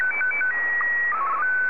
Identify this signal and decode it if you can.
SSTV (slow-scan television), Martin 1.
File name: sstv